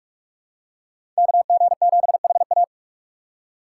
Morse code, heard as a keyed tone, about 42 wpm, 700 Hz. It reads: KG7HA
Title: USA Call Signs - 5 Characters - 6x Character Spacing - 45wpm